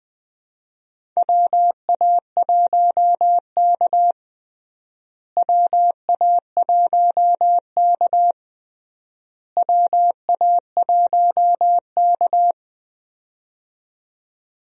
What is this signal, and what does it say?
Morse code, heard as a keyed tone, about 20 wpm, 700 Hz. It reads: WA1K WA1K WA1K
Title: USA Call Signs - 4 Characters - Repeated 3x - 20wpm